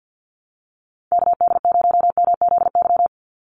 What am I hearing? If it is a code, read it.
Morse code, heard as a keyed tone, about 50 wpm, 700 Hz. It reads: KD0MZY
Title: USA Call Signs 50wpm